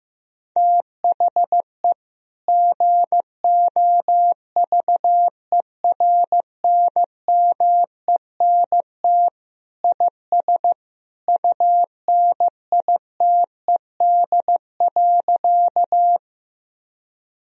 Morse code, heard as a keyed tone, about 15 wpm, 700 Hz. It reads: THE GOVERNMENT IS UNITED.
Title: Sentences from Top 1000 Words - Review Part 4 - 15wpm